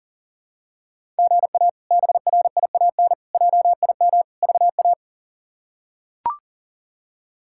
Morse code, heard as a keyed tone, about 40 wpm, 700 Hz. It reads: GA BRIAN JIM VA
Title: K1USN SST QSO Practice - 40wpm